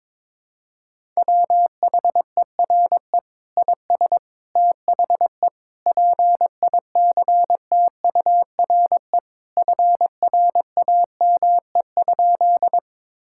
Morse code, heard as a keyed tone, about 22 wpm, 700 Hz. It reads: WHERE IS THE PICTURE FRAME?
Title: Sentences from Top 1000 Words - Review Part 6 - 22wpm